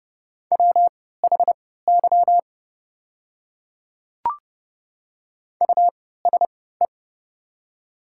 Morse code, heard as a keyed tone, about 31 wpm, spaced as for 17 wpm, 700 Hz. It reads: WHY USE
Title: 3x Character Spacing - Top 200 Words - 30wpm